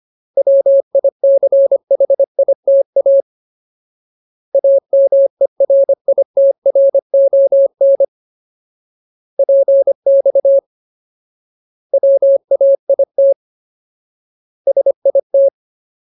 Morse code, heard as a keyed tone, about 25 wpm, 550 Hz. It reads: WICHITA AMERITRON PX WAIT SIT
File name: Ultimate Word List / 25WPM Ultimate